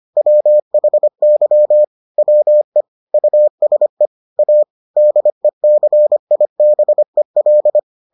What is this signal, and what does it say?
Morse code, heard as a keyed tone, about 25 wpm, 600 Hz. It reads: WHY WE USE A DECIBEL